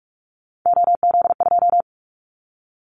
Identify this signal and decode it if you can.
Morse code, heard as a keyed tone, about 46 wpm, 700 Hz. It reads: OZ2
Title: Callsign Prefixes for Top 50 DXCC - 45wpm